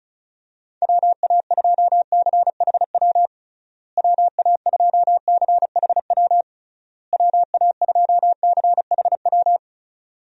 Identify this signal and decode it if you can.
Morse code, heard as a keyed tone, about 36 wpm, 700 Hz. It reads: WA2CHW WA2CHW WA2CHW
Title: USA Call Signs - 6 Characters - Repeated 3x - 35wpm